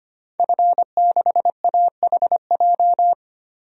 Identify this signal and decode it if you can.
Morse code, heard as a keyed tone, about 25 wpm, 700 Hz. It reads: F6AHJ
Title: DXCC-100 Call Signs - 5 Characters - 25wpm